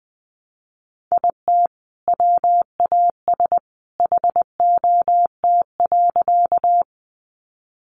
Morse code, heard as a keyed tone, about 19 wpm, 700 Hz. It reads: IT WAS HOT.
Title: Sentences from Top 500 Words 20wpm